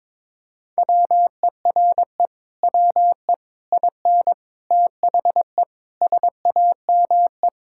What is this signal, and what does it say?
Morse code, heard as a keyed tone, about 22 wpm, 700 Hz. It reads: WERE WE IN THE SAME
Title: Sentences from Top 1000 Words - Review Part 6 - 22wpm